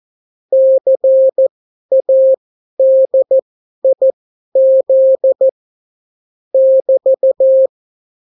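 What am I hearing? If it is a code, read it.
Morse code, heard as a keyed tone, about 14 wpm, spaced as for 11 wpm, 530 Hz. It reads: CADIZ <BT>
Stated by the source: Example Spanish CW QSO from a Morse learning book, 14 wpm characters with 8 wpm spacing.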